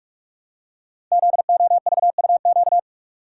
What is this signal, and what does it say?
Morse code, heard as a keyed tone, about 44 wpm, 700 Hz. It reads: ZOVUQ
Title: Code Groups - 5 Letters - 45wpm